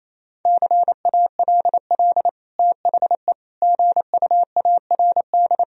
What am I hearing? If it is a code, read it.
Morse code, heard as a keyed tone, about 28 wpm, 700 Hz. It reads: CALL THE GUARD
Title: Sentences from Top 1000 Words - Review Part 4 - 28wpm